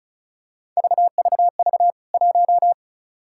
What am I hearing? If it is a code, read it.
Morse code, heard as a keyed tone, about 36 wpm, 700 Hz. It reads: VVV 1
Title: K1USN SST QSO Practice 35wpm